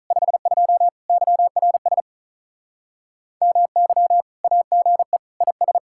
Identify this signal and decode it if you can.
Morse code, heard as a keyed tone, about 46 wpm, 700 Hz. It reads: 52 YRS MY AGE IS